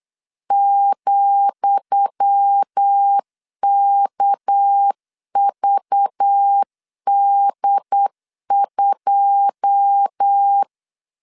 Morse code, heard as a keyed tone, about 8 wpm, 800 Hz. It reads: ,KVD2
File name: LEZIONI / 10 SIGNS / 10D COMA 8wpm 3749